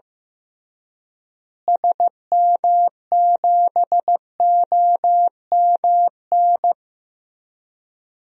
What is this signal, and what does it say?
Morse code, heard as a keyed tone, about 15 wpm, 700 Hz. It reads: SM7OMN
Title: DXCC-100 Call Signs - 6 Characters - Repeated 3x - 15wpm